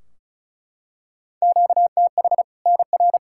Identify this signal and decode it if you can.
Morse code, heard as a keyed tone, about 36 wpm, 700 Hz. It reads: QTH NR
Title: Practice QSO - Exchange 1 - Beginning - 35wpm